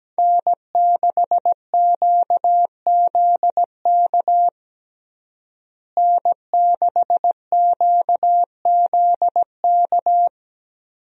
Morse code, heard as a keyed tone, about 17 wpm, 700 Hz. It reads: N6QZK N6QZK
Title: USA Call Signs - 5 Characters - Repeated 3x - 17wpm